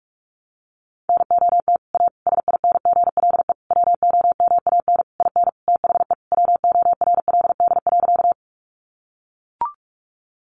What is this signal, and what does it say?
Morse code, heard as a keyed tone, about 44 wpm, 700 Hz. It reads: NOT A SINGLE WOMAN IN THE WORLD.
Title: Sentences from Top 1100 Words 45wpm